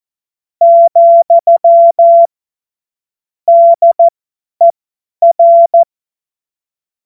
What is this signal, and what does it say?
Morse code, heard as a keyed tone, about 14 wpm, spaced as for 10 wpm, 680 Hz. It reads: , DER
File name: Morsekurs / Witze / Witz 02. Mai 2023 50 F70